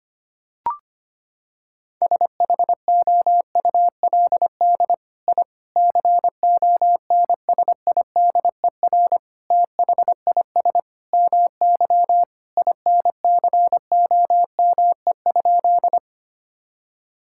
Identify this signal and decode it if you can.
Morse code, heard as a keyed tone, about 25 wpm, 700 Hz. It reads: SHOULD I CONSIDER THIS MY INCOME?
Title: Sentences from Top 1000 Words - Review Part 5 - 25wpm